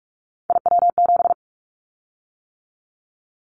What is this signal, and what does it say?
Morse code, heard as a keyed tone, about 44 wpm, 700 Hz. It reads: IW7
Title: Callsign Prefixes for Top 50 DXCC - 45wpm